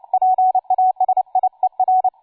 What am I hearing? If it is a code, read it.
Morse code, heard as a keyed tone, about 29 wpm, 750 Hz, with radio noise. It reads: PASIER
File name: cw forum / tu2 f5ldy 18mhz 16h46